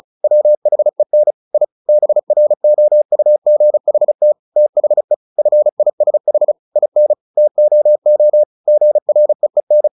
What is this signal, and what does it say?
Morse code, heard as a keyed tone, about 34 wpm, 600 Hz. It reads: WHEN I BROUGHT THE FISH IN TOO GREEN
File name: the old man and the sea / the old man and the sea 1 35wpm